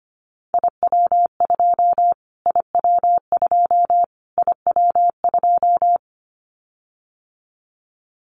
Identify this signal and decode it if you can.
Morse code, heard as a keyed tone, about 25 wpm, 700 Hz. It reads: IW2 IW2 IW2
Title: Callsign Prefixes for Top 25 DXCC - Repeated 3x - 25wpm